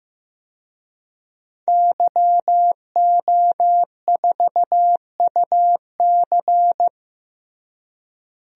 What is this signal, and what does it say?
Morse code, heard as a keyed tone, about 15 wpm, 700 Hz. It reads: YO4UC
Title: DXCC-100 Call Signs - 5 Characters - Repeated 2x - 15wpm